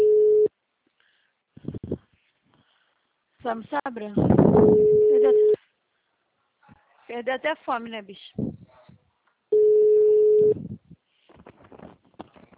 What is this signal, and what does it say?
Ringback tone (Europe, 425 Hz), with speech.
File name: out- -2033- - - .41170